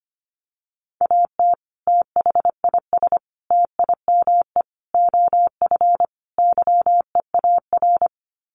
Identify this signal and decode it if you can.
Morse code, heard as a keyed tone, about 25 wpm, 700 Hz. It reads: AT THIS TIME OF YEAR
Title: Sets of 5 Words Encore2 25wpm